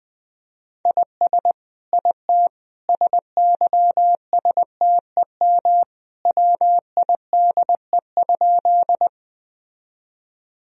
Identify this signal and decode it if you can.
Morse code, heard as a keyed tone, about 21 wpm, 700 Hz. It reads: IS IT SYSTEM WIDE?
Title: Sentences from Top 1000 Words - Review Part 3 - 20wpm